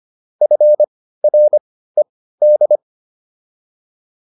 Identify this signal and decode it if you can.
Morse code, heard as a keyed tone, about 25 wpm, spaced as for 15 wpm, 600 Hz. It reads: FRED
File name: QSO404 15